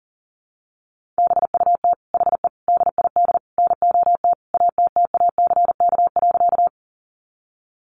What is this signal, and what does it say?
Morse code, heard as a keyed tone, about 38 wpm, 700 Hz. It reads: BUT HE DID NOT ATTACK.
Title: Sentences from Top 1500 Words 40wpm